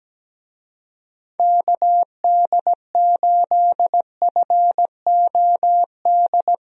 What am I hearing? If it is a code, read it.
Morse code, heard as a keyed tone, about 17 wpm, 700 Hz. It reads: KD8FOD
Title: USA Call Signs - 6 Characters - Repeated 2x - 17wpm